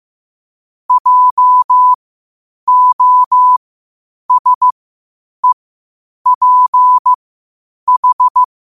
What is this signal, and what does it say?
Morse code, heard as a keyed tone, about 15 wpm, spaced as for 8 wpm, 1000 Hz. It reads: JOSEPH
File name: 8wpm